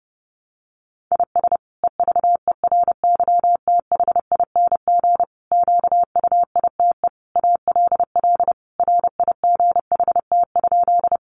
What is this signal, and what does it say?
Morse code, heard as a keyed tone, about 30 wpm, 700 Hz. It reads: IS EVERYTHING QUITE ALL RIGHT?
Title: Sentences from Top 700 Words 30wpm